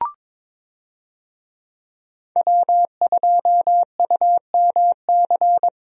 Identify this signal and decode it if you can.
Morse code, heard as a keyed tone, about 21 wpm, 700 Hz. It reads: W2UMC
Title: USA Call Signs - 5 Characters - Repeated 3x - 22wpm